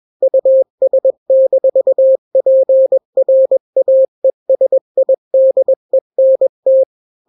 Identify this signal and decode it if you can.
Morse code, heard as a keyed tone, about 21 wpm, 530 Hz. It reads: US-PRAESIDENT
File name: daily / 20wpm